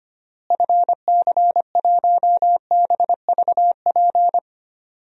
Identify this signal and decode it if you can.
Morse code, heard as a keyed tone, about 25 wpm, 700 Hz. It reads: FC1BVP
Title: DXCC-100 Call Signs - 6 Characters - Repeated 3x - 25wpm